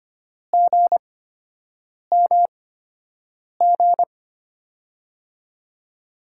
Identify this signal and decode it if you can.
Morse code, heard as a keyed tone, about 25 wpm, spaced as for 7 wpm, 700 Hz. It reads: GMG
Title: USA Call Signs - 6 Characters - 8x Character Spacing - 25wpm